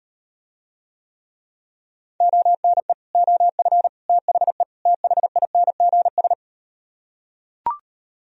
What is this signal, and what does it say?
Morse code, heard as a keyed tone, about 38 wpm, 700 Hz. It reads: ONE OF THE THINGS